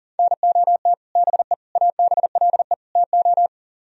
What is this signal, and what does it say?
Morse code, heard as a keyed tone, about 52 wpm, 700 Hz. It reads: NOT BE ABLE TO